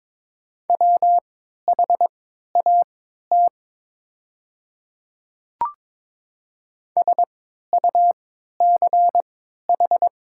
Morse code, heard as a keyed tone, about 22 wpm, spaced as for 12 wpm, 690 Hz. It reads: WHAT SUCH
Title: 3x Character Spacing - Top 200 Words - 22wpm